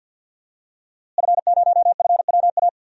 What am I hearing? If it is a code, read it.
Morse code, heard as a keyed tone, about 49 wpm, 700 Hz. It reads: F0FWA
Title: DXCC-100 Call Signs - 5 Characters - Repeated 2x - 50wpm